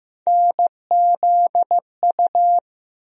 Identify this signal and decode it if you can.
Morse code, heard as a keyed tone, about 15 wpm, 700 Hz. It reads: NZU
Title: USA Call Sign Postfixes - 3 Letters 15wpm